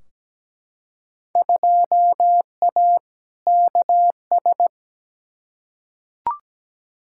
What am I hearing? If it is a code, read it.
Morse code, heard as a keyed tone, about 17 wpm, 700 Hz. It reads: 2A KS
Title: Field Day Exchanges - 17wpm